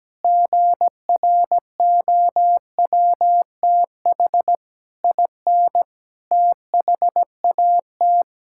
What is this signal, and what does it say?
Morse code, heard as a keyed tone, about 17 wpm, 700 Hz. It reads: GROWTH IN THAT
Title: Sentences from Top 1000 Words - Review Part 5 - 17wpm